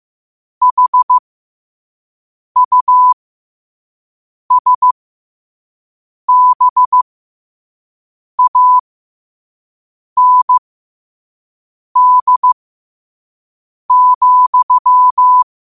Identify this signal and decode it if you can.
Morse code, heard as a keyed tone, about 15 wpm, spaced as for 5 wpm, 1000 Hz. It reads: HUSBAND,
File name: qotdfiles / 5wpm